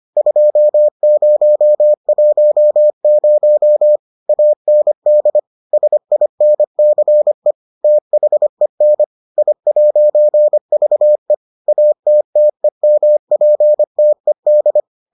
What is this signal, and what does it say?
Morse code, heard as a keyed tone, about 25 wpm, 600 Hz. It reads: 2010 AND SINCE THEN I'VE ATTEMPTED